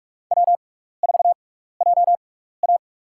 Morse code, heard as a keyed tone, about 46 wpm, spaced as for 15 wpm, 700 Hz. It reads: W4JA
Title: USA Call Signs - 6x Character Spacing - 45wpm